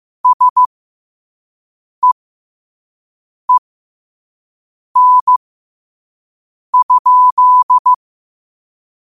Morse code, heard as a keyed tone, about 15 wpm, spaced as for 5 wpm, 1000 Hz. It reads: SEEN?
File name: qotdfiles / 5wpm